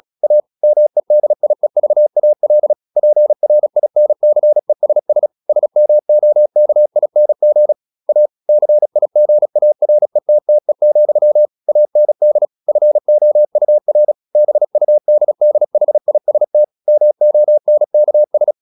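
Morse code, heard as a keyed tone, about 36 wpm, 600 Hz. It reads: A MEDIEVAL PRINCESS SMOKING A CIGARETTE, AND FOUR BUDDHIST MONKS
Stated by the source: Morse practice at 36 wpm, text of Ken Follett's novel Never, from CW med Gnister.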